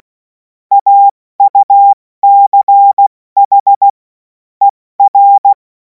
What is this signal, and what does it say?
Morse code, heard as a keyed tone, about 16 wpm, 800 Hz. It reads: AUCH ER
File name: Morsekurs / Witze / Witz 17. März 2026 70 F80